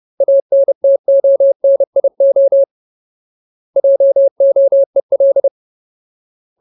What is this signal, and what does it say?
Morse code, heard as a keyed tone, about 30 wpm, 550 Hz. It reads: ANTONIO JOEL
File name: Ultimate Word List / 30WPM Ultimate